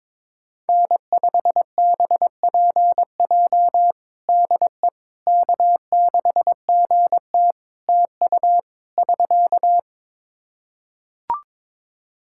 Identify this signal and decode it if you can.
Morse code, heard as a keyed tone, about 22 wpm, 700 Hz. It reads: N5BPJ DE K6GT TU <SK>
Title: K1USN SST QSO Practice 22wpm